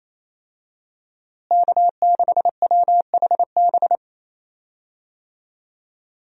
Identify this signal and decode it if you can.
Morse code, heard as a keyed tone, about 28 wpm, 700 Hz. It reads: K6WHB
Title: USA Call Signs - 5 Characters - Repeated 2x - 28wpm